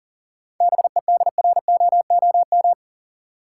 Morse code, heard as a keyed tone, about 40 wpm, 700 Hz. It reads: BEDROOM
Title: Seven Letter Words - 40wpm